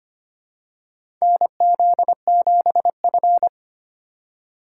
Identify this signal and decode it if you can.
Morse code, heard as a keyed tone, about 25 wpm, 700 Hz. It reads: NZ7F
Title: USA Call Signs - 3x Character Spacing - 25wpm